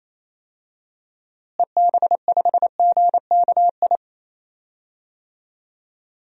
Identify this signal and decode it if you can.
Morse code, heard as a keyed tone, about 28 wpm, 700 Hz. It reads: EB5GKI
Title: DXCC-100 Call Signs - 6 Characters - Repeated 2x - 28wpm